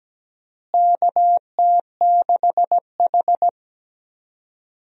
Morse code, heard as a keyed tone, about 17 wpm, 700 Hz. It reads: KT6H
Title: USA Call Signs - 4 Characters - Repeated 3x - 17wpm